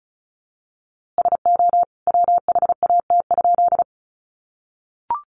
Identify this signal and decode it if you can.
Morse code, heard as a keyed tone, about 34 wpm, 700 Hz. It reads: SO WHAT?
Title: A Sound Of Thunder 35wpm